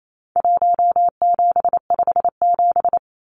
Morse code, heard as a keyed tone, about 28 wpm, 700 Hz. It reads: 1757
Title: Numbers - Contest Style - 4-Digit 28wpm